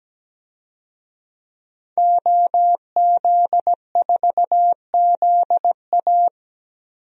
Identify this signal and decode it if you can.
Morse code, heard as a keyed tone, about 17 wpm, 700 Hz. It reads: OZ4ZA
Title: DXCC-100 Call Signs - 5 Characters - Repeated 2x - 17wpm